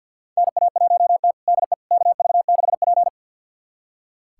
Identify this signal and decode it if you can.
Morse code, heard as a keyed tone, about 46 wpm, 700 Hz. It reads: NA1T DE KV6P
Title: USA Callsign DE Callsign - 4 Characters - 50wpm